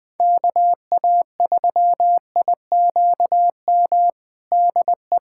Morse code, heard as a keyed tone, about 20 wpm, 700 Hz. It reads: KA3IQM DE
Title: USA Callsign DE Callsign - 6 Characters - 20wpm